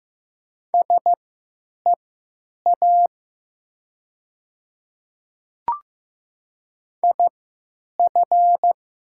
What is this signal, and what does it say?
Morse code, heard as a keyed tone, about 15 wpm, spaced as for 9 wpm, 700 Hz. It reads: SEA IF
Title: 3x Character Spacing - Top 300 Words - 15wpm